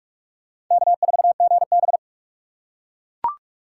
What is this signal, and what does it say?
Morse code, heard as a keyed tone, about 42 wpm, 700 Hz. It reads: K4GB
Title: USA Call Signs - 4 Characters - 4x Character Spacing - 45wpm